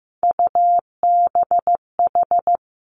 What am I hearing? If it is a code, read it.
Morse code, heard as a keyed tone, about 15 wpm, 700 Hz. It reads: UBH
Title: Three Challenging Letter-Number Combinations 15wpm